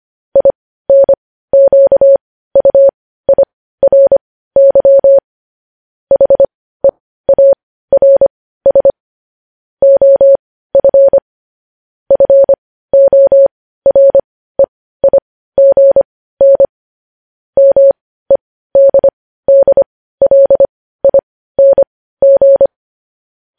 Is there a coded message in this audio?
Yes: Morse code from 0.4 to 22.7 s, heard as a keyed tone, about 25 wpm, spaced as for 15 wpm, 550 Hz. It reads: INQUIRY HEARS OF FOREIGN MEDDLING